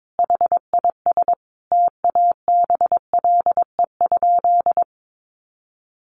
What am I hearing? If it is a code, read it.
Morse code, heard as a keyed tone, about 22 wpm, 700 Hz. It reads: HIS TABLE?
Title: Sentences from Top 1500 Words 22wpm